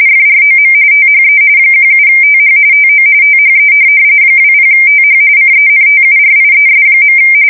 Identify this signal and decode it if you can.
RTTY, about 45.45 baud. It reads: FROM THE 17TH TO THE 20TH CENTURIES.
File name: fsk170 / USS-New-Jersey